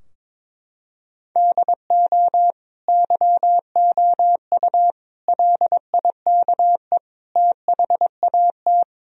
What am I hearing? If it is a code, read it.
Morse code, heard as a keyed tone, about 22 wpm, 700 Hz. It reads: DO YOU LIKE THAT
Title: Sentences from Top 1000 Words - Review Part 3 - 22wpm